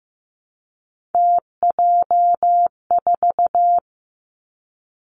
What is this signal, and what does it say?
Morse code, heard as a keyed tone, about 15 wpm, 700 Hz. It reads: TJ4
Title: Three Challenging Letter-Number Combinations 15wpm